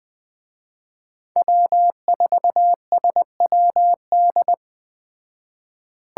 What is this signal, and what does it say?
Morse code, heard as a keyed tone, about 20 wpm, 700 Hz. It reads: W4SWD
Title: Speed Racing - USA Call Signs 17wpm